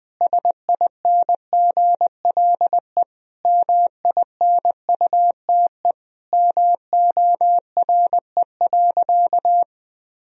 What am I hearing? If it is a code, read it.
Morse code, heard as a keyed tone, about 20 wpm, 700 Hz. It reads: SINGLE MINUTE MORE.
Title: Sentences from Top 1000 Words - Review Part 4 - 20wpm